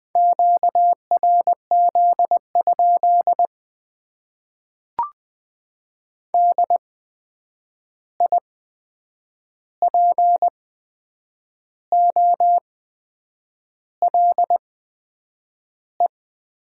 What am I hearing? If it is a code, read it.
Morse code, heard as a keyed tone, about 20 wpm, spaced as for 8 wpm, 700 Hz. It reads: QRZ? DIPOLE
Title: Top 90 QSO Elements - 8x Character Spacing - 20wpm